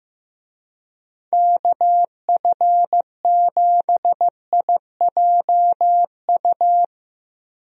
Morse code, heard as a keyed tone, about 15 wpm, 700 Hz. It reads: KF7IJU
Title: USA Call Signs - 6 Characters - Repeated 3x - 15wpm